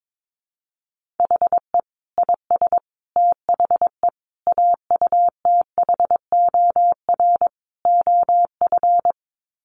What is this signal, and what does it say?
Morse code, heard as a keyed tone, about 22 wpm, 700 Hz. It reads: HE IS THE AUTHOR OF
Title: Sets of 5 Words Encore2 22wpm